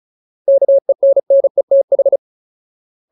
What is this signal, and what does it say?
Morse code, heard as a keyed tone, about 34 wpm, 550 Hz. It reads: KENNETH
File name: Ultimate Word List / 35WPM Ultimate